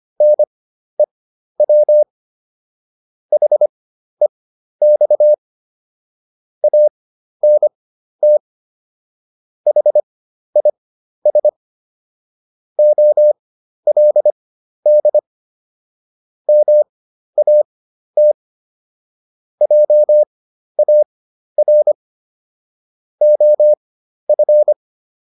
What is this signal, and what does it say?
Morse code, heard as a keyed tone, about 25 wpm, spaced as for 12 wpm, 600 Hz. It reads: NEW HEX ANT HIS OLD MAT JAR OF